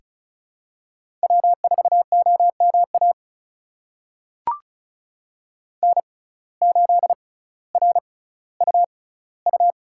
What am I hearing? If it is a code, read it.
Morse code, heard as a keyed tone, about 36 wpm, spaced as for 18 wpm, 700 Hz. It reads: W4OMA N8RUU
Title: USA Call Signs - 5 Characters - 6x Character Spacing - 35wpm